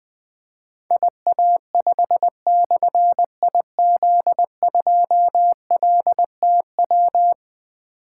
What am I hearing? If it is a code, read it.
Morse code, heard as a keyed tone, about 20 wpm, 700 Hz. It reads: IA5/IZ2LTW
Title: Difficult Call Signs 20wpm